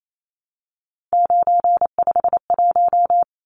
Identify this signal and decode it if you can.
Morse code, heard as a keyed tone, about 28 wpm, 700 Hz. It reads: 951
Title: Numbers - 3-Digit 28wpm